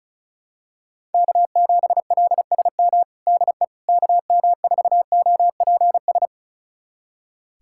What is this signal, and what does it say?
Morse code, heard as a keyed tone, about 34 wpm, 700 Hz. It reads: K7LSM DE KM4OPS
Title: USA Callsign DE Callsign 35wpm